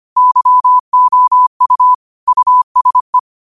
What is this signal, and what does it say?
Morse code, heard as a keyed tone, about 25 wpm, 1000 Hz. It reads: YOU USE
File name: qotdfiles / 25wpm